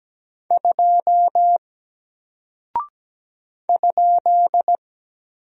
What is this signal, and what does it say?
Morse code, heard as a keyed tone, about 17 wpm, 700 Hz. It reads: 2 ?
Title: Single Character (Letter:Number:Punctuation:Prosign) Alt 17wpm